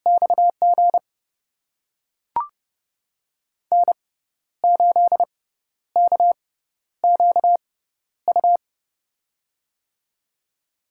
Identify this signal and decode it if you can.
Morse code, heard as a keyed tone, about 30 wpm, spaced as for 12 wpm, 700 Hz. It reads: XG N8KQU
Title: USA Call Signs - 5 Characters - 6x Character Spacing - 30wpm